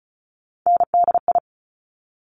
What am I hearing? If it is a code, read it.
Morse code, heard as a keyed tone, about 36 wpm, 700 Hz. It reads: NDI
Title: Top Three Letter Combinations in Words 35wpm